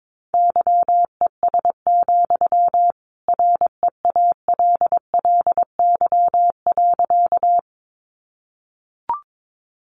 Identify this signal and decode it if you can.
Morse code, heard as a keyed tone, about 22 wpm, 700 Hz. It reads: YES, REALLY.
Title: Sentences from Top 600 Words 22wpm